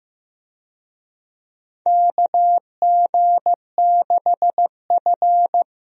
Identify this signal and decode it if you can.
Morse code, heard as a keyed tone, about 15 wpm, 700 Hz. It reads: KG6F
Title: USA Call Signs - 4 Characters - Repeated 2x - 15wpm